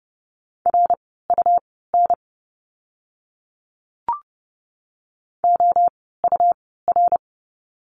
Morse code, heard as a keyed tone, about 30 wpm, spaced as for 17 wpm, 700 Hz. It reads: RUN OUR
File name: Lesson 17f - Letter u - Words 30wpm